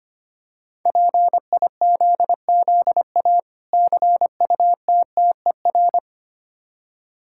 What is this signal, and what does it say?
Morse code, heard as a keyed tone, about 25 wpm, 700 Hz. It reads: PIZZA CUTTER
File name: practice / 25wpm / Difficult Words with Hints - 25wpm